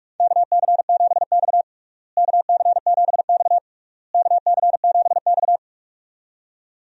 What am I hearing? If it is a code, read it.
Morse code, heard as a keyed tone, about 42 wpm, 700 Hz. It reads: KC7X KC7X KC7X
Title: USA Call Signs - 4 Characters - Repeated 3x - 45wpm